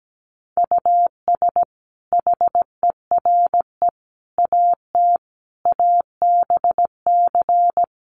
Morse code, heard as a keyed tone, about 17 wpm, 700 Hz. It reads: US HERE AT ABC
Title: Sets of 5 Words 17wpm